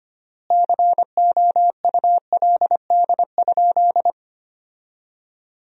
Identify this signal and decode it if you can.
Morse code, heard as a keyed tone, about 25 wpm, 700 Hz. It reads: COULD?
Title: 3x Word-Spacing - Sentences from Top 400 Words - 25wpm